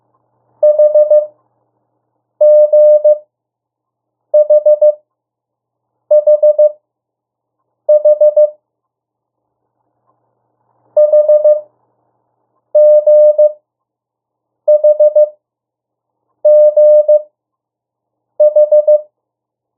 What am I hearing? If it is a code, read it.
Morse code, heard as a keyed tone, about 15 wpm, spaced as for 6 wpm, 600 Hz. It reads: HGHHH HGHGH
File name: morse mp3 / 02 urok 1